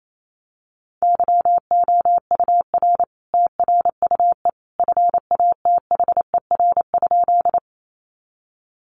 Morse code, heard as a keyed tone, about 28 wpm, 700 Hz. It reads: YOUR TRUE FATHER?
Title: Sentences from Top 400 Words 28wpm